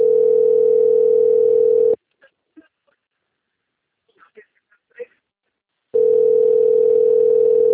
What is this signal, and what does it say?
Ringback tone (North America), 440 + 480 Hz.
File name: out- - - - - .24891